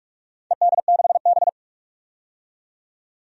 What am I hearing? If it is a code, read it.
Morse code, heard as a keyed tone, about 47 wpm, 700 Hz. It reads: ED6B
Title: DXCC-100 Call Signs - 4 Characters - 45wpm